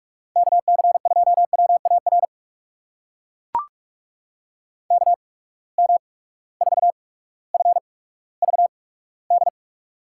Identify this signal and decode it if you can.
Morse code, heard as a keyed tone, about 42 wpm, spaced as for 19 wpm, 700 Hz. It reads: KC2WAR KM4FVD
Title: USA Call Signs - 6 Characters - 8x Character Spacing - 45wpm